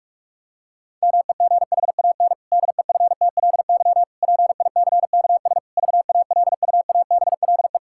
Morse code, heard as a keyed tone, about 46 wpm, 700 Hz. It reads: MEGHAN DEFTLY PICKS VALUABLE
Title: Pangram Challenge v2 - 45wpm